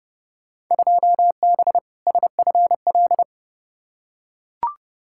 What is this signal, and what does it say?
Morse code, heard as a keyed tone, about 30 wpm, 700 Hz. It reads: 2B SFL
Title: Field Day Exchanges - 30wpm